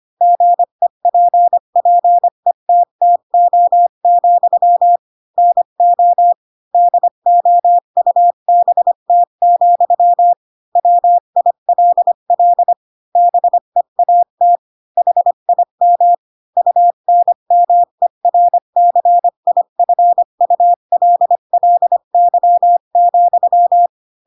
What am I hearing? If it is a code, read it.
Morse code, heard as a keyed tone, about 25 wpm, 700 Hz. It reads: GEPPETTO, NO DOUBT, WILL BEAT HIM UNMERCIFULLY,
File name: Pinocchio / Pinocchio ch3 20 25 ik1hge